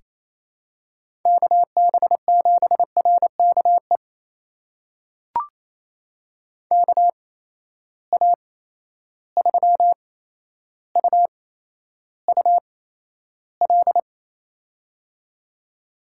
Morse code, heard as a keyed tone, about 28 wpm, spaced as for 12 wpm, 700 Hz. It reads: KB7RKE KA3UUL
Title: USA Call Signs - 6 Characters - 8x Character Spacing - 28wpm